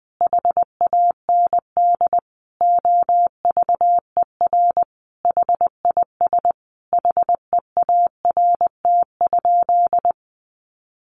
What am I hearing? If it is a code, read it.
Morse code, heard as a keyed tone, about 20 wpm, 700 Hz. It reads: HAND OVER HIS HEART?
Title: Sentences from Top 1200 Words 20wpm